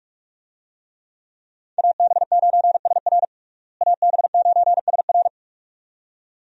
Morse code, heard as a keyed tone, about 46 wpm, 700 Hz. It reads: AB9SR AB9SR
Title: USA Call Signs - 5 Characters - Repeated 2x - 45wpm